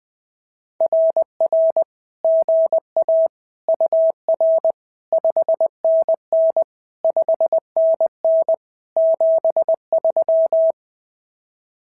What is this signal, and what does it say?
Morse code, heard as a keyed tone, about 20 wpm, 650 Hz. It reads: RR GA UR 5NN 5NN 73
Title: SOTA QSO Practice 20wpm